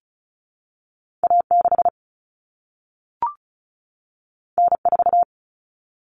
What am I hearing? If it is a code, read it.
Morse code, heard as a keyed tone, about 36 wpm, 700 Hz. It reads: A6 N4
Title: Common Cut Numbers - 2-Digit 35wpm